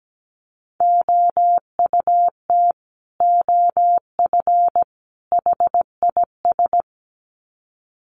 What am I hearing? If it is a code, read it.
Morse code, heard as a keyed tone, about 17 wpm, 700 Hz. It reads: OUT OF HIS
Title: Sets of 3 Words 17wpm